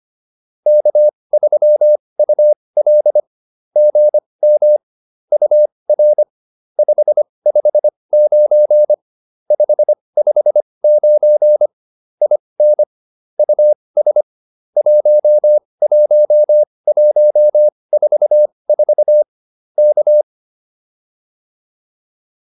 Morse code, heard as a keyed tone, about 25 wpm, spaced as for 20 wpm, 600 Hz. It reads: K3UL GM UR 559 559 IN US 11144 K